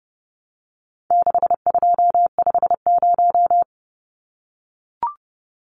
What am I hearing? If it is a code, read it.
Morse code, heard as a keyed tone, about 29 wpm, 700 Hz. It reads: 6250
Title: Numbers - 4-Digit 30wpm